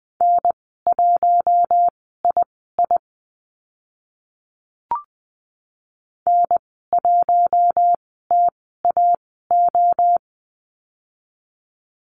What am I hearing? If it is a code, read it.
Morse code, heard as a keyed tone, about 20 wpm, spaced as for 14 wpm, 700 Hz. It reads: N1II N1TAO
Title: 042 Lesson 8g - Number 1 - Call Signs 20wpm